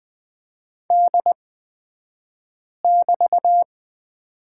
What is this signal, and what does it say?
Morse code, heard as a keyed tone, about 20 wpm, spaced as for 5 wpm, 700 Hz. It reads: D<BT>
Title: 249 Lesson 40d - BK - Two Characters 20wpm